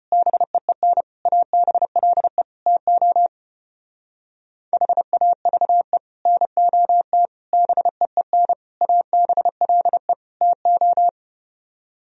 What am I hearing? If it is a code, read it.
Morse code, heard as a keyed tone, about 33 wpm, 700 Hz. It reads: BEEN ABLE TO HAVE NOT BEEN ABLE TO